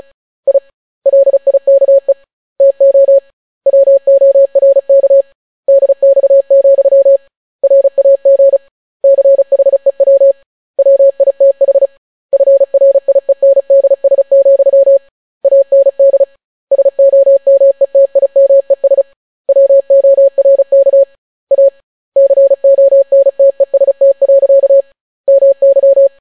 Morse code, heard as a keyed tone, about 36 wpm, 550 Hz. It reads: I LIKE TO WORK DX, RAG CHEW WITH FRIENDS, AND SOMETIMES WORK A CONTEST. MY